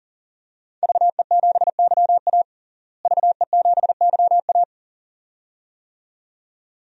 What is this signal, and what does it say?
Morse code, heard as a keyed tone, about 40 wpm, 700 Hz. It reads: VE7YA VE7YA
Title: DXCC-100 Call Signs - 5 Characters - Repeated 2x - 40wpm